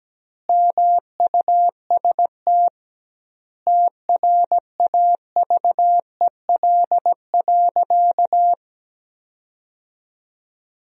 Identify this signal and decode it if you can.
Morse code, heard as a keyed tone, about 17 wpm, 700 Hz. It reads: MUST TRAVEL.
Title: 2x Word-Spacing - Sentences from Top 400 Words - 17wpm